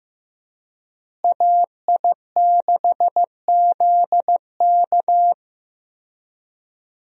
Morse code, heard as a keyed tone, about 15 wpm, 700 Hz. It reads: AI6ZK
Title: USA Call Signs - 5 Characters - Repeated 3x - 15wpm